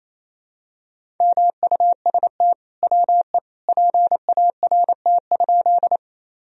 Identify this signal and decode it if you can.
Morse code, heard as a keyed tone, about 28 wpm, 700 Hz. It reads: MUST WE PART?
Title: Sentences from Top 1000 Words - Review Part 1 - 28wpm